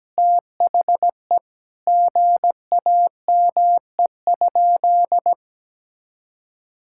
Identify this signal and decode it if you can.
Morse code, heard as a keyed tone, about 17 wpm, 700 Hz. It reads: THE GAME?
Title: Sentences from Top 1000 Words - Review Part 6 - 17wpm